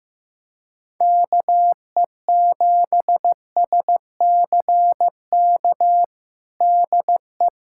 Morse code, heard as a keyed tone, about 15 wpm, 700 Hz. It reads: KE7SCK DE
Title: USA Callsign DE Callsign - 6 Characters - 15wpm